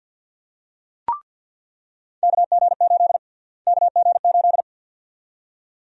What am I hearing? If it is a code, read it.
Morse code, heard as a keyed tone, about 49 wpm, 700 Hz. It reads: KG8 KG8
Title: USA Call Sign Prefixes - Repeated 2x - 50wpm